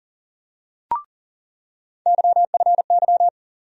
Morse code, heard as a keyed tone, about 40 wpm, 700 Hz. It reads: YFY
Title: USA Call Sign Postfixes - 3 Letters 40wpm